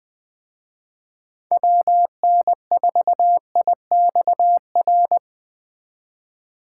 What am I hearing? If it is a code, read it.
Morse code, heard as a keyed tone, about 21 wpm, 700 Hz. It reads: WN4IXR